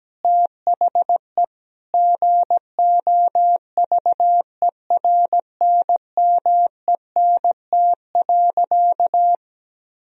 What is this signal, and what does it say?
Morse code, heard as a keyed tone, about 17 wpm, 700 Hz. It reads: THE GOVERNMENT.
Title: Sentences from Top 1000 Words - Review Part 3 - 17wpm